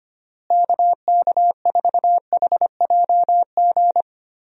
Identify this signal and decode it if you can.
Morse code, heard as a keyed tone, about 25 wpm, 700 Hz. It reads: KK4HJG
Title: USA Call Signs - 6 Characters - Repeated 3x - 25wpm